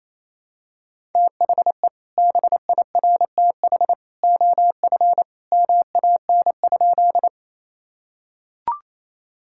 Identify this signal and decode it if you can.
Morse code, heard as a keyed tone, about 28 wpm, 700 Hz. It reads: THE BIRTH OF MAN?
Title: Sentences from Top 1000 Words - Review Part 5 - 28wpm